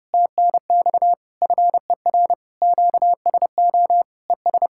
Morse code, heard as a keyed tone, about 31 wpm, 700 Hz. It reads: TNX FER QSO ES